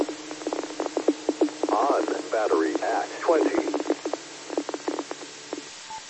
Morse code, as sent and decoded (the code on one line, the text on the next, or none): none